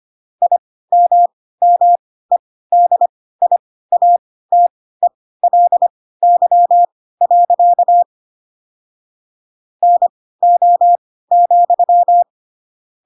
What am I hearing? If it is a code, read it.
Morse code, heard as a keyed tone, about 25 wpm, spaced as for 16 wpm, 700 Hz. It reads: IMMEDIATELY. NO,
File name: Pinocchio / Pinocchio ch1 10 25 ik1hge